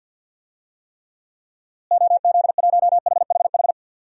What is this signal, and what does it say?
Morse code, heard as a keyed tone, about 47 wpm, 700 Hz. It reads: OZ1HHH
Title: DXCC-100 Call Signs - 6 Characters - Repeated 2x - 50wpm